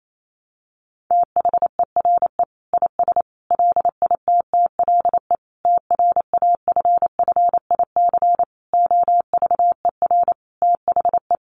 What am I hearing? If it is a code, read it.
Morse code, heard as a keyed tone, about 28 wpm, 700 Hz. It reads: THERE IS LITTLE TRAFFIC OVER THE
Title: Sentences from Top 1300 Words 28wpm